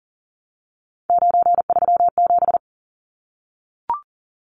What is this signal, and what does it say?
Morse code, heard as a keyed tone, about 40 wpm, 700 Hz. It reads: 937
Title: Numbers - 3-Digit 40wpm